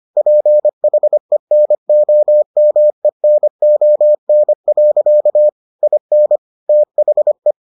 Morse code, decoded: PHENOMENON. IN THE

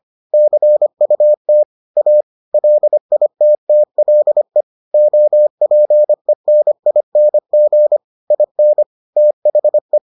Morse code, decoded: CUT A LITTLE OPENING IN THE

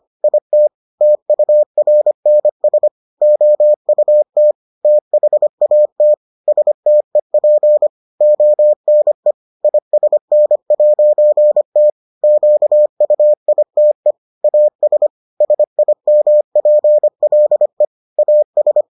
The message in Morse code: IT TURNS OUT THAT STEP ONE ISN'T QUITE AS SIMPLE AS

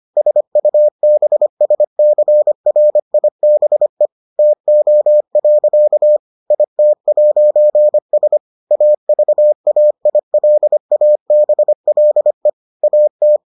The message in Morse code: SUBSCRIBE TO. IT'S AVAILABLE AT

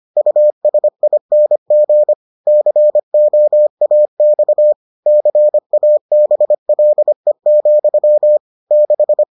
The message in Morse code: USING COAX CABLE, 6

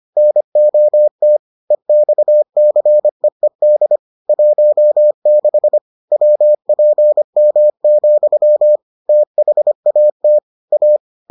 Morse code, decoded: NOT EXCEED 16 WPM, THAT A

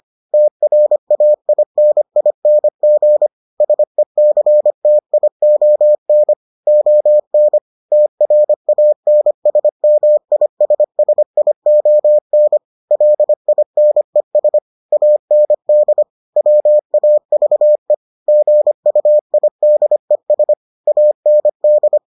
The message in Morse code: TRAINING SECTION ON TRANSMISSION LINES AND WAVE GUIDES AND

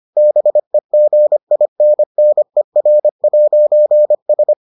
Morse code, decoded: BEGINNER'S

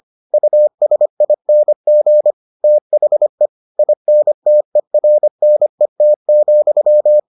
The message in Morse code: USING THE INTERNET,

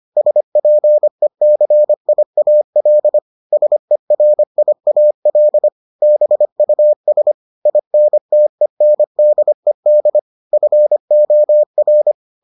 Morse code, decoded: SPECIAL SERIAL BUS INTENDED FOR

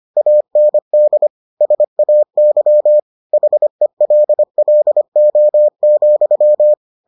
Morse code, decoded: AND SAY HELLO,

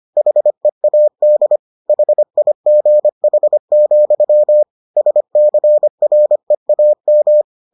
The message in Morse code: HEAD HIGH, SCREAM

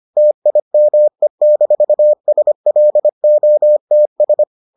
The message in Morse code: TIME-SLOTS

